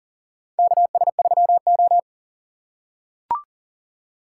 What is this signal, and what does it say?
Morse code, heard as a keyed tone, about 40 wpm, 700 Hz. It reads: KS3O
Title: USA Call Signs - 4 Characters - 4x Character Spacing - 40wpm